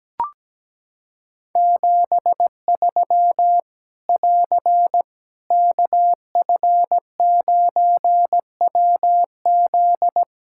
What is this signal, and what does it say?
Morse code, heard as a keyed tone, about 17 wpm, 700 Hz. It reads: 73 <AR> KF9WZ
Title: Practice QSO - Exchange 4 - Ending - 17wpm